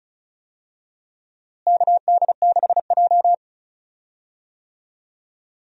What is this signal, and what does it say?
Morse code, heard as a keyed tone, about 36 wpm, 700 Hz. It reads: KD6J
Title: USA Call Signs - 4 Characters - Repeated 3x - 35wpm